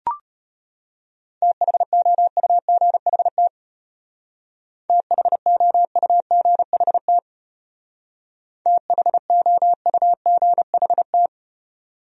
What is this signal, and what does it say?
Morse code, heard as a keyed tone, about 33 wpm, 700 Hz. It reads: THOUGHT THOUGHT THOUGHT